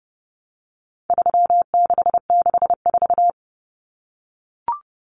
Morse code, heard as a keyed tone, about 30 wpm, 700 Hz. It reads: 3664
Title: Common Cut Numbers - 4-Digit 30wpm